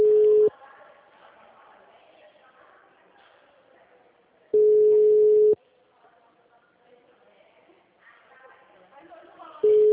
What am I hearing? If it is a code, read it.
Ringback tone (Europe, 425 Hz), with speech.